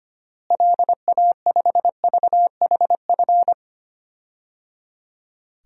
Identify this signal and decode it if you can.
Morse code, heard as a keyed tone, about 25 wpm, 700 Hz. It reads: LA5VHF
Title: DXCC-100 Call Signs - 6 Characters - Repeated 3x - 25wpm